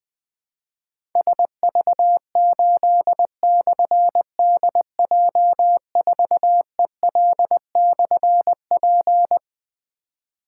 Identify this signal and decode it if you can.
Morse code, heard as a keyed tone, about 20 wpm, 700 Hz. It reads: SV8/DJ4EL/P
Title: Difficult Call Signs 20wpm